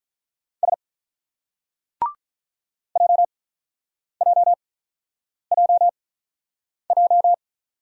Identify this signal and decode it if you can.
Morse code, heard as a keyed tone, about 42 wpm, spaced as for 8 wpm, 700 Hz. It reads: S JJJJ
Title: Speed-Racing Single Letters 35wpm